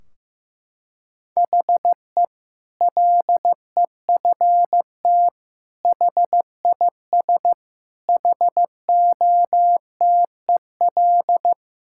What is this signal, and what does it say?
Morse code, heard as a keyed tone, about 15 wpm, 700 Hz. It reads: HE LEFT HIS HOTEL
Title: Sentences from Top 1600 Words 15wpm